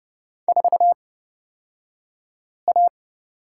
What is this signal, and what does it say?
Morse code, heard as a keyed tone, about 30 wpm, spaced as for 5 wpm, 700 Hz. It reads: 4A
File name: course / 30:10wpm / 235 Lesson 38d - Letter q - Two Characters 30wpm